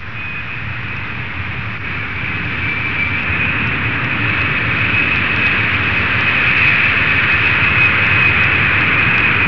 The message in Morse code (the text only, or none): none